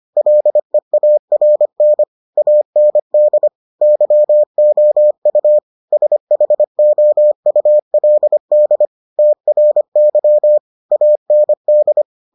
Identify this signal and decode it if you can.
Morse code, heard as a keyed tone, about 25 wpm, 600 Hz. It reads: LEARN AND YOU SHOULD TRY AND